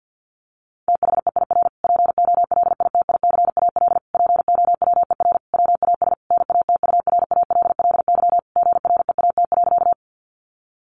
Morse code, heard as a keyed tone, about 49 wpm, 700 Hz. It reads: THEIR POLITICAL POWER WAS NATURALLY GREAT.
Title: Sentences from Top 1300 Words 50wpm